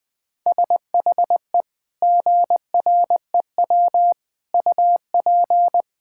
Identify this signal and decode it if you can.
Morse code, heard as a keyed tone, about 20 wpm, 700 Hz. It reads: SHE GREW UP
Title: Sentences with 2-Word Phrases - 20wpm